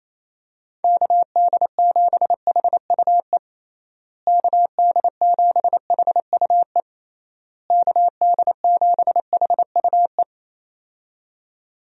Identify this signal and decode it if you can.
Morse code, heard as a keyed tone, about 28 wpm, 700 Hz. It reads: KD7HUE KD7HUE KD7HUE
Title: USA Call Signs - 6 Characters - Repeated 3x - 28wpm